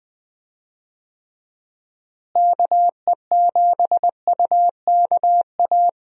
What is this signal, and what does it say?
Morse code, heard as a keyed tone, about 20 wpm, 700 Hz. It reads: KE7UKA
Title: USA Call Signs - 6 Characters - Repeated 2x - 20wpm